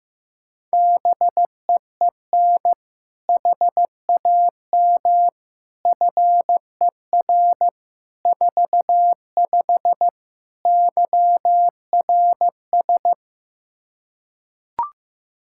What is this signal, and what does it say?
Morse code, heard as a keyed tone, about 15 wpm, 700 Hz. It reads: BEEN HAM FER 45 YRS
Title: Practice QSO - Exchange 3 - More Info - 15wpm